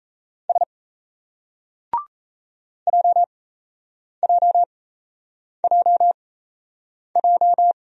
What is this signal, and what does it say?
Morse code, heard as a keyed tone, about 34 wpm, spaced as for 8 wpm, 700 Hz. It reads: S JJJJ
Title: Speed-Racing Single Letters 28wpm